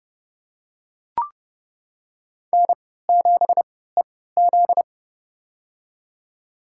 Morse code, heard as a keyed tone, about 29 wpm, spaced as for 17 wpm, 700 Hz. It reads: N7EZ
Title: USA Call Signs - 4 Characters - 3x Character Spacing - 30wpm